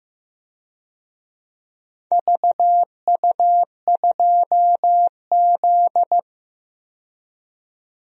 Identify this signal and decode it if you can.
Morse code, heard as a keyed tone, about 15 wpm, 700 Hz. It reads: VU2Z
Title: DXCC-100 Call Signs - 4 Characters - 15wpm